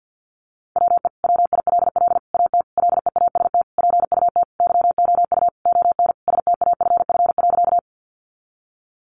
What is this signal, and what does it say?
Morse code, heard as a keyed tone, about 50 wpm, 700 Hz. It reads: WE WILL AT LEAST PUT YOU ON STAFF.
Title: Sentences from Top 1200 Words 50wpm